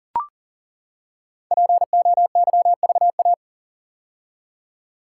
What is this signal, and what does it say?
Morse code, heard as a keyed tone, about 40 wpm, 700 Hz. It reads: POYVA
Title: Code Groups - 5 Letters - 40wpm